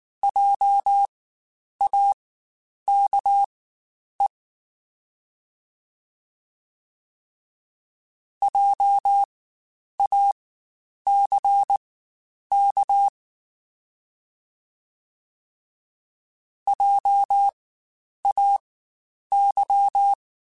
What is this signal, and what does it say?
Morse code, heard as a keyed tone, about 19 wpm, spaced as for 9 wpm, 800 Hz. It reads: JAKE JACK JAY
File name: Spaced Alphas / Spaced Alphas mp3s / Js sp @18